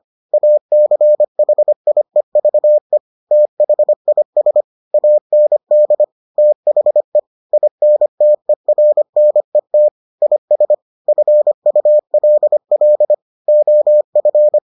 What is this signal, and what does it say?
Morse code, heard as a keyed tone, about 25 wpm, 600 Hz. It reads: ACHIEVE THIS AND THE INTERNET IS FULL OF